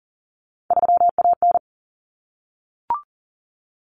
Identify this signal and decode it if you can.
Morse code, heard as a keyed tone, about 40 wpm, 700 Hz. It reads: 3AN
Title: Common Cut Numbers - 3-Digit 40wpm